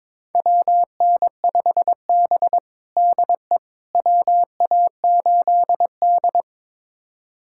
Morse code, heard as a keyed tone, about 22 wpm, 700 Hz. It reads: WN5B DE WA8D
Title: USA Callsign DE Callsign - 4 Characters - 22wpm